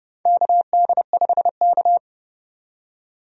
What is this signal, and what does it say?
Morse code, heard as a keyed tone, about 32 wpm, 700 Hz. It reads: KD5K
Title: USA Call Signs - 4 Characters - 30wpm